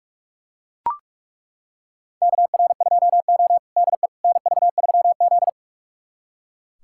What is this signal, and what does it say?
Morse code, heard as a keyed tone, about 41 wpm, 700 Hz. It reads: KR2O DE NV3Z
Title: USA Callsign DE Callsign - 4 Characters - 45wpm